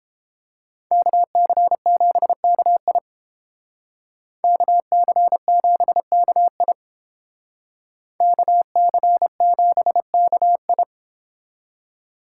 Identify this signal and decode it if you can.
Morse code, heard as a keyed tone, about 30 wpm, 700 Hz. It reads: KC7KI KC7KI KC7KI